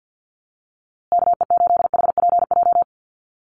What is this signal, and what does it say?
Morse code, heard as a keyed tone, about 50 wpm, 700 Hz. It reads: KE8HPJ
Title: USA Call Signs Repeated 3x 50wpm